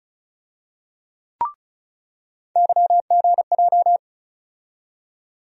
Morse code, heard as a keyed tone, about 36 wpm, 700 Hz. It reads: YGJ
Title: USA Call Sign Postfixes - 3 Letters 35wpm